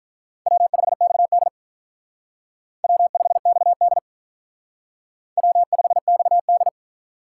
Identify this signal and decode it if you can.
Morse code, heard as a keyed tone, about 45 wpm, 700 Hz. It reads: W5XD W5XD W5XD